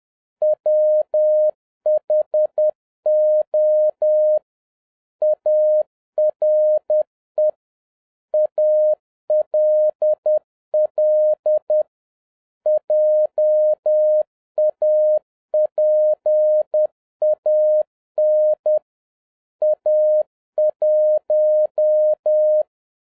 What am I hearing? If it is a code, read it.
Morse code, heard as a keyed tone, about 10 wpm, 610 Hz. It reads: WHO ARE ALL JAPAN A1
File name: 2011-07-mp3 / n46 JOHN K3TN 10wpm